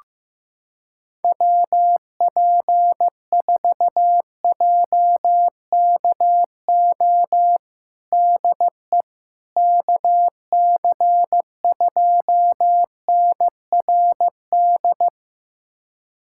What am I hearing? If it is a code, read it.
Morse code, heard as a keyed tone, about 15 wpm, 700 Hz. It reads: WP4JKO DE KC2NRD
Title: USA Callsign DE Callsign 15wpm